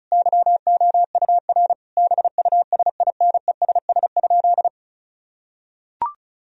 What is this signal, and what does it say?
Morse code, heard as a keyed tone, about 34 wpm, 700 Hz. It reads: YOUR BUSINESS?
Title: Sentences from Top 1000 Words - Review Part 4 - 35wpm